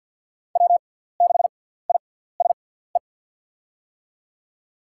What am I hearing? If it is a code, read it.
Morse code, heard as a keyed tone, about 48 wpm, spaced as for 17 wpm, 700 Hz. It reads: W6ISE
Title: USA Call Signs - 5 Characters - 6x Character Spacing - 50wpm